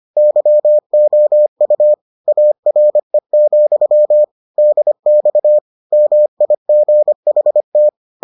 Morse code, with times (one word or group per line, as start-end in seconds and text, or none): 0.2-1.9 YOU
2.3-4.2 ARE,
4.6-5.6 DX
5.9-7.9 MIGHT